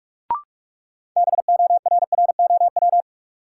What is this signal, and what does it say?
Morse code, heard as a keyed tone, about 45 wpm, 700 Hz. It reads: BORROW
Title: Sentences from Top 1500 Words - New Word Drill - 45wpm